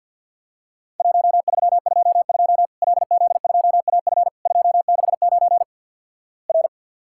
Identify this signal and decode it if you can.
Morse code, heard as a keyed tone, about 50 wpm, 700 Hz. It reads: 1322 LZ2AF 269 R
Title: Worked All Europe DX Contest - QTC Exchange with Repeats - 50wpm